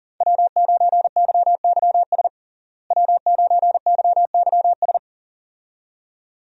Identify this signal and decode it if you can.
Morse code, heard as a keyed tone, about 40 wpm, 700 Hz. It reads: W9YYS W9YYS
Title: USA Call Signs - 5 Characters - Repeated 2x - 40wpm